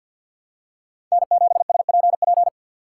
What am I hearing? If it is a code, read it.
Morse code, heard as a keyed tone, about 53 wpm, 700 Hz. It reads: N7SPP
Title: USA Call Signs - 5 Characters - 3x Character Spacing - 50wpm